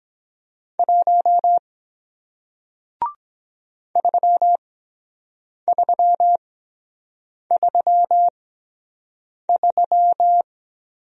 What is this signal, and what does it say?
Morse code, heard as a keyed tone, about 22 wpm, spaced as for 7 wpm, 700 Hz. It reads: 1 3333